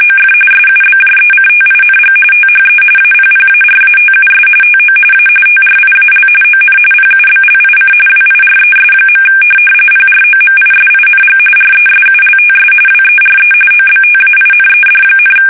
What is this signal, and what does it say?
RTTY, about 45.45 baud. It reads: OF SUBJECTING THE APPARATUS TO A TRIAL ON PUBLIC WIRES. PROMISING RESULTS WERE OBTAINED,